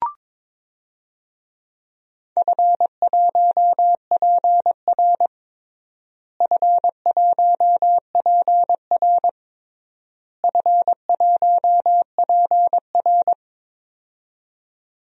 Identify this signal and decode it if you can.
Morse code, heard as a keyed tone, about 22 wpm, 700 Hz. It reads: F1PR F1PR F1PR
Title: DXCC-100 Call Signs - 4 Characters - Repeated 3x - 22wpm